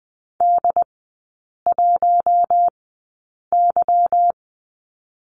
Morse code, heard as a keyed tone, about 20 wpm, spaced as for 8 wpm, 690 Hz. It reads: D1Y
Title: 159 Lesson 26e - Letter g - Three Characters 20wpm